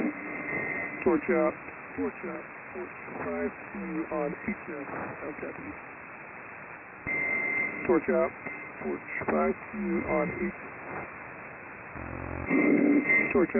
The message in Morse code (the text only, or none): none